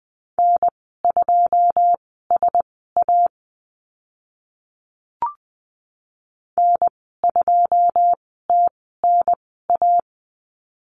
Morse code, heard as a keyed tone, about 20 wpm, spaced as for 14 wpm, 700 Hz. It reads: N2SA N2TNA
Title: 082 Lesson 14g - Number 2 - Call Signs 20wpm